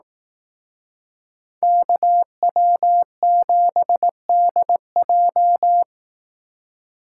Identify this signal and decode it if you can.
Morse code, heard as a keyed tone, about 18 wpm, 700 Hz. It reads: KW7DJ
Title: Speed Racing - USA 2x2 Call Signs 15wpm